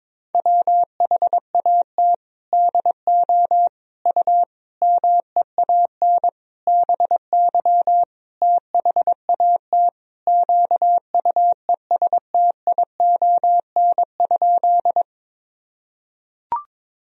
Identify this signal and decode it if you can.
Morse code, heard as a keyed tone, about 22 wpm, 700 Hz. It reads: WHAT DO U MEAN BY THAT QUESTION?
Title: Sentences from Top 400 Words and Abbreviations 22wpm